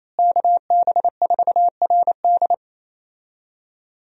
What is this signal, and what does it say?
Morse code, heard as a keyed tone, about 27 wpm, 700 Hz. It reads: KB4RD
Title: USA Call Signs - 5 Characters - Repeated 2x - 28wpm